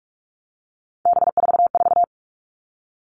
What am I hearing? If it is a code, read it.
Morse code, heard as a keyed tone, about 44 wpm, 700 Hz. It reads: B44
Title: All Cut Numbers - 3-Digit 45wpm